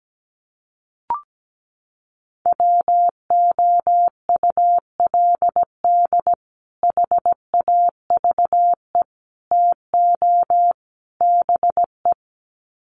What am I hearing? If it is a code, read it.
Morse code, heard as a keyed tone, about 17 wpm, 700 Hz. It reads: WOULD HAVE TO BE
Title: Sets of 4 Words 17wpm